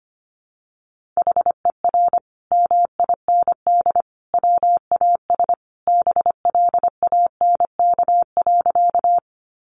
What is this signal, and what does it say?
Morse code, heard as a keyed tone, about 25 wpm, 700 Hz. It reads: HER MIND WAS BLANK.
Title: Sentences from Top 1500 Words 25wpm